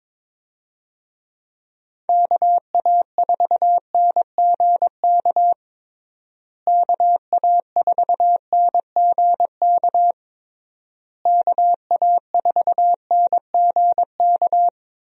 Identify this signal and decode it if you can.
Morse code, heard as a keyed tone, about 22 wpm, 700 Hz. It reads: KA4NGK KA4NGK KA4NGK
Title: USA Call Signs - 6 Characters - Repeated 3x - 22wpm